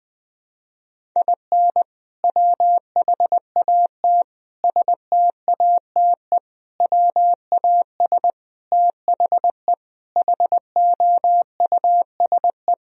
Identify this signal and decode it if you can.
Morse code, heard as a keyed tone, about 21 wpm, 700 Hz. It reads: IN WHAT STATE WAS THE HOUSE
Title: Sentences from Top 1000 Words - Review Part 2 - 20wpm